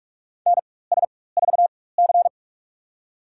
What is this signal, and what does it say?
Morse code, heard as a keyed tone, about 46 wpm, spaced as for 21 wpm, 700 Hz. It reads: NS4C
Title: USA Call Signs - 4 Characters - 4x Character Spacing - 45wpm